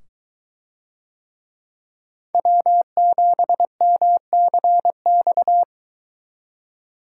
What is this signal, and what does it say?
Morse code, heard as a keyed tone, about 22 wpm, 700 Hz. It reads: W7MCX